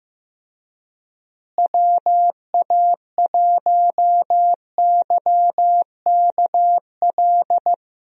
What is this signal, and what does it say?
Morse code, heard as a keyed tone, about 15 wpm, 700 Hz. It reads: WA1YKL
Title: USA Call Signs - 6 Characters - Repeated 2x - 15wpm